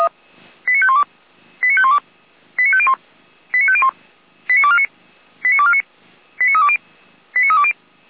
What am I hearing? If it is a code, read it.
Five-tone selective call (ZVEI), heard as quick runs of five tones, tones of about 70 ms. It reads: 89512 89512 89501 89501 89249 89249 89320 89320